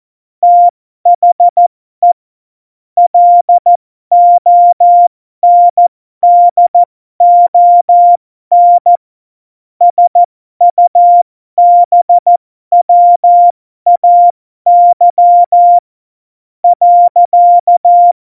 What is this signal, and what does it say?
Morse code, heard as a keyed tone, about 14 wpm, spaced as for 12 wpm, 700 Hz. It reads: THE LONDON SUBWAY .
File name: Morsekurs / Witze